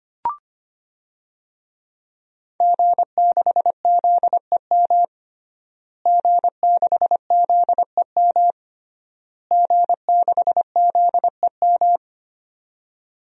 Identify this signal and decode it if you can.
Morse code, heard as a keyed tone, about 25 wpm, 700 Hz. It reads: G6ZEM G6ZEM G6ZEM
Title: DXCC-100 Call Signs - 5 Characters - Repeated 3x - 25wpm